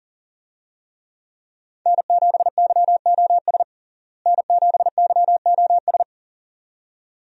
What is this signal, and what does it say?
Morse code, heard as a keyed tone, about 40 wpm, 700 Hz. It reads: N7YOS N7YOS
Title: USA Call Signs - 5 Characters - Repeated 2x - 40wpm